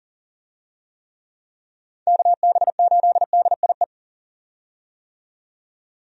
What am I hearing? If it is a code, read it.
Morse code, heard as a keyed tone, about 40 wpm, 700 Hz. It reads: KB8DIE
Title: USA Call Signs - 6 Characters - 40wpm